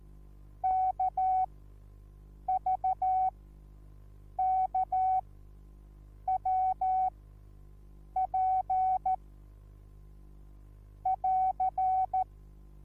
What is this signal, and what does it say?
Morse code, heard as a keyed tone, about 14 wpm, spaced as for 6 wpm, 750 Hz. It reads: KVKWP <AR>
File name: Onlinelehrgang / morsen / DJ4UF Morsen21